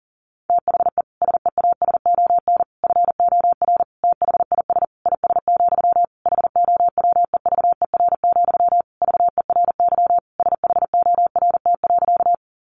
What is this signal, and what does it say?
Morse code, heard as a keyed tone, about 40 wpm, 700 Hz. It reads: THE SEASON FOR THIS IS, HOWEVER, VERY SHORT.
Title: Sentences from Top 1400 Words 40wpm